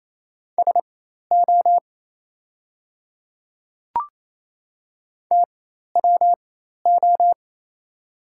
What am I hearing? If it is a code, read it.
Morse code, heard as a keyed tone, about 28 wpm, spaced as for 13 wpm, 700 Hz. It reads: SO TWO
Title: 4x Character Spacing - Top 500 Words - 28wpm